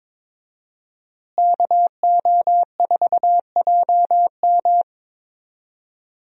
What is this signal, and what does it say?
Morse code, heard as a keyed tone, about 22 wpm, 700 Hz. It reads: KO4JM
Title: USA Call Signs - 5 Characters - Repeated 2x - 22wpm